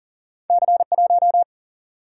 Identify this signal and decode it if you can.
Morse code, heard as a keyed tone, about 39 wpm, 700 Hz. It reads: C 1
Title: Single-Letter Single-Number 40wpm